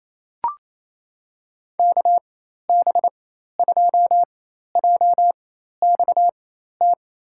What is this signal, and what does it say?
Morse code, heard as a keyed tone, about 28 wpm, spaced as for 13 wpm, 700 Hz. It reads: KB2JXT
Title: USA Call Signs - 6 Characters - 4x Character Spacing - 28wpm